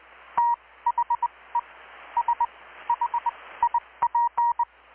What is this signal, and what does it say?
Morse code, heard as a keyed tone, about 20 wpm, spaced as for 15 wpm, 950 Hz. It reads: THE SHIP
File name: blog / swl / 6-abril-2019-12695-USB-KPS